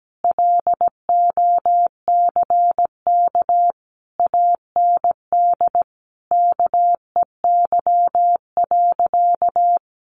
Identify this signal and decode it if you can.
Morse code, heard as a keyed tone, about 17 wpm, 700 Hz. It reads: LOCK AND KEY.
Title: Sentences from Top 1400 Words 17wpm